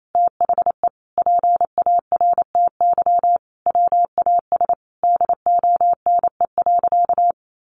Morse code, heard as a keyed tone, about 28 wpm, 700 Hz. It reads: THE PARTY WAS DONE.
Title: Sentences from Top 1400 Words 28wpm